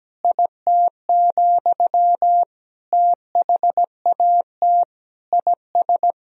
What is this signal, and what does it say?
Morse code, heard as a keyed tone, about 17 wpm, 700 Hz. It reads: IT, THAT IS
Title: Sentences from Top 400 Words and Abbreviations 17wpm